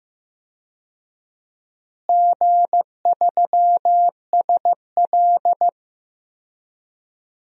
Morse code, heard as a keyed tone, about 15 wpm, 700 Hz. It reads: G3SL
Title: DXCC-100 Call Signs - 4 Characters - Repeated 2x - 15wpm